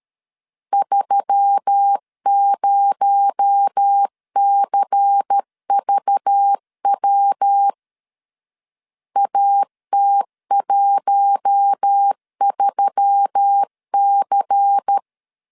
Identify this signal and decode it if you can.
Morse code, heard as a keyed tone, about 13 wpm, 800 Hz. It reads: 30CVW AT13C
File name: CW / DET / 1 9 1500C S C 12wpm 3811